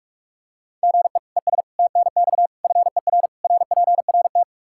Morse code, heard as a keyed tone, about 45 wpm, 700 Hz. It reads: GE ES TNX FER RPRT
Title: K1USN SST QSO Practice 45wpm